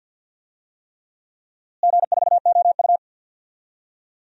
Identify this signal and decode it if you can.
Morse code, heard as a keyed tone, about 49 wpm, 700 Hz. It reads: G4OU
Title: DXCC-100 Call Signs - 4 Characters - Repeated 3x - 50wpm